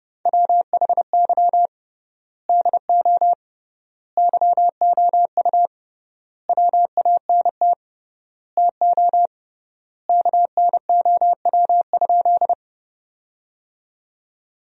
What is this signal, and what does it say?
Morse code, heard as a keyed tone, about 32 wpm, 700 Hz. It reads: WHY DO YOU WANT TO KNOW?
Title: 3x Word-Spacing - Sentences from Top 200 Words - 30wpm